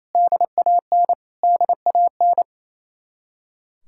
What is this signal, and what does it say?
Morse code, heard as a keyed tone, about 35 wpm, 700 Hz. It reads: DAN DAN